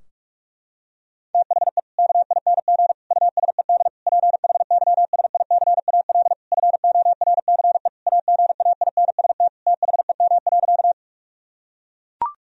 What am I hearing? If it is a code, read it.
Morse code, heard as a keyed tone, about 41 wpm, 700 Hz. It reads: THE KING USED PHYSICAL FORCE AGAINST THEM.
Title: Sentences from Top 1000 Words - Review Part 4 - 45wpm